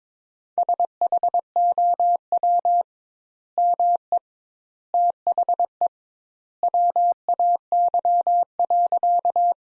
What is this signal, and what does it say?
Morse code, heard as a keyed tone, about 22 wpm, 700 Hz. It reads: SHOW ME THE WAY.
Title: Sentences Top 200 Words - 22wpm